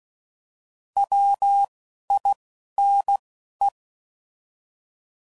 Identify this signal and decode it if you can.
Morse code, heard as a keyed tone, about 16 wpm, spaced as for 11 wpm, 800 Hz. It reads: WINE
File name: Prefixes / By First Letter / Ws